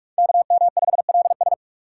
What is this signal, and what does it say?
Morse code, heard as a keyed tone, about 39 wpm, 700 Hz. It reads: KM5LS